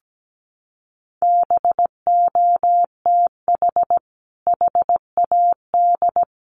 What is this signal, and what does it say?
Morse code, heard as a keyed tone, about 17 wpm, 700 Hz. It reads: BOTH HAD
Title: Sentences from Top 1200 Words 17wpm